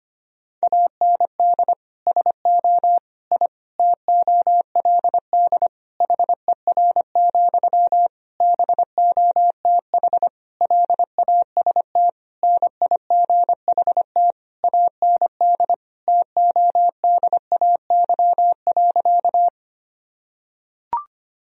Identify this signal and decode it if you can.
Morse code, heard as a keyed tone, about 25 wpm, 700 Hz. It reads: AND SO I TOLD HER, BOTH LAST NIGHT AND TODAY.
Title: Sentences from Top 1000 Words - Review Part 4 - 25wpm